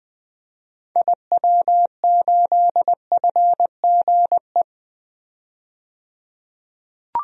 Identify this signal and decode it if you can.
Morse code, heard as a keyed tone, about 20 wpm, 700 Hz. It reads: IW8FGE
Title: DXCC-100 Call Signs - 6 Characters - 20wpm